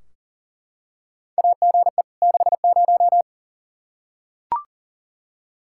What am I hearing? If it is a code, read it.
Morse code, heard as a keyed tone, about 40 wpm, 700 Hz. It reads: AGE 60
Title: Practice QSO - Exchange 3 - More Info - 40wpm